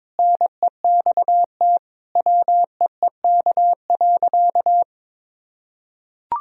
Morse code, heard as a keyed tone, about 22 wpm, 700 Hz. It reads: NEXT WEEK.
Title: Sentences from Top 1000 Words - Review Part 4 - 22wpm